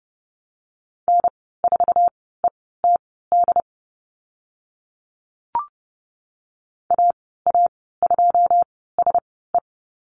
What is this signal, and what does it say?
Morse code, heard as a keyed tone, about 30 wpm, spaced as for 17 wpm, 700 Hz. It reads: N4ETD AA2SE
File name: Lesson 15g - Number 5 - Call Signs 30wpm